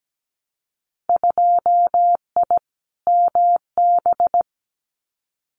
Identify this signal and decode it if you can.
Morse code, heard as a keyed tone, about 17 wpm, 700 Hz. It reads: 2I MB
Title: Winter Field Day Exchanges Encore 17wpm